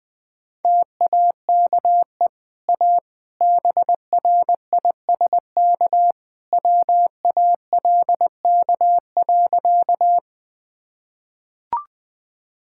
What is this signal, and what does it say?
Morse code, heard as a keyed tone, about 20 wpm, 700 Hz. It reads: TAKE A BRISK WALK.
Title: Easy Sentences with Hints - 20wpm